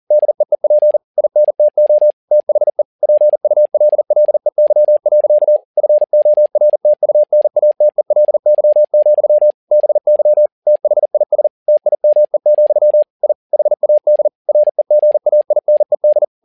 Morse code, heard as a keyed tone, about 40 wpm, 600 Hz. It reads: DEEP INTO THE PULLEY. FORTUNATELY, BY THIS TIME, I HAD REGAINED
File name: learning center / tower trouble / tower touble 40wpm